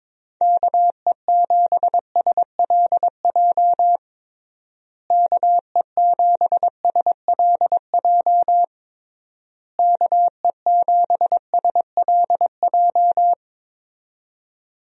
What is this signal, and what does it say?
Morse code, heard as a keyed tone, about 22 wpm, 700 Hz. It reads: KE7SLJ KE7SLJ KE7SLJ
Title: USA Call Signs - 6 Characters - Repeated 3x - 22wpm